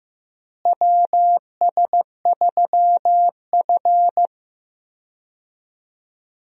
Morse code, heard as a keyed tone, about 15 wpm, 700 Hz. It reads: WS3F
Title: USA Call Signs - 4 Characters - Repeated 2x - 15wpm